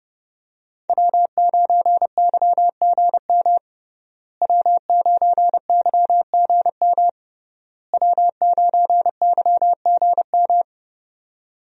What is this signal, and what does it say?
Morse code, heard as a keyed tone, about 30 wpm, 700 Hz. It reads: W9YGM W9YGM W9YGM
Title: USA Call Signs - 5 Characters - Repeated 3x - 30wpm